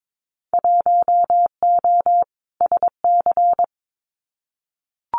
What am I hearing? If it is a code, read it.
Morse code, heard as a keyed tone, about 22 wpm, 700 Hz. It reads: 1O SC
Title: Winter Field Day Exchanges 22wpm